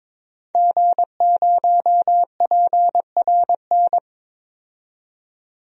Morse code, heard as a keyed tone, about 22 wpm, 700 Hz. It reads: G0PRN
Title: DXCC-100 Call Signs - 5 Characters - 22wpm